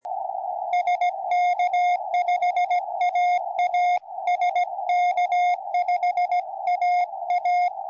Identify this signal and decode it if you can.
Morse code, heard as a keyed tone, about 17 wpm, 710 Hz, with radio noise. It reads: SK5AA SK5AA
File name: tropo / sk5aa